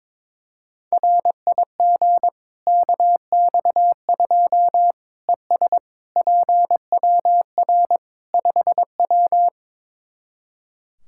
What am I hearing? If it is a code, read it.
Morse code, heard as a keyed tone, about 22 wpm, 700 Hz. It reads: RIG KX2 ES PWR 5W
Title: Practice QSO - Exchange 2 - Info - 22wpm